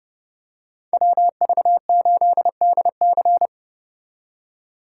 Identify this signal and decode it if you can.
Morse code, heard as a keyed tone, about 29 wpm, 700 Hz. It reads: WV8DC